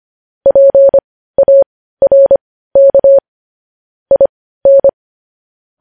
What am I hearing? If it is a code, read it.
Morse code, heard as a keyed tone, about 25 wpm, spaced as for 15 wpm, 550 Hz. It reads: PARK IN
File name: podcast / News headlines-1x-25x15